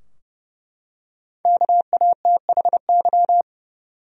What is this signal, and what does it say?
Morse code, heard as a keyed tone, about 30 wpm, 700 Hz. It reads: KATHY
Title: USA Common Names - 30wpm